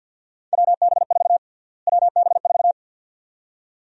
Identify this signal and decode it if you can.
Morse code, heard as a keyed tone, about 52 wpm, 700 Hz. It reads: WB4 WB4
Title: USA Call Sign Prefixes - Repeated 2x - 50wpm